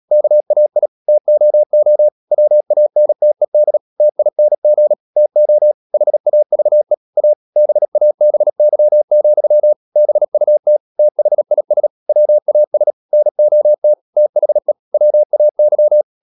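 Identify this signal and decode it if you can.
Morse code, heard as a keyed tone, about 37 wpm, 600 Hz. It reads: KAI TOO WANTED TING TO HAVE A BABY, BUT THIS WAS NOT THE WAY
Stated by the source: Morse code practice at 35–39 wpm sending English text of Ken Follett's novel Never, from CW med Gnister.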